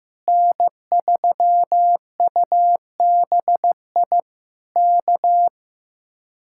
Morse code, heard as a keyed tone, about 15 wpm, 700 Hz. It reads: N3UBI K
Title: Speed-Racing Common QSO Phrases Encore - 15wpm